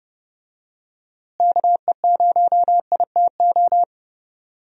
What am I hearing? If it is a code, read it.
Morse code, heard as a keyed tone, about 31 wpm, 700 Hz. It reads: KE0ITO
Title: USA Call Signs - 6 Characters - Repeated 2x - 30wpm